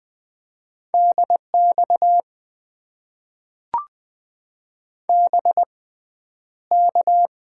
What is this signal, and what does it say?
Morse code, heard as a keyed tone, about 20 wpm, spaced as for 10 wpm, 700 Hz. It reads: DX BK
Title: Top 90 QSO Elements - 6x Character Spacing - 20wpm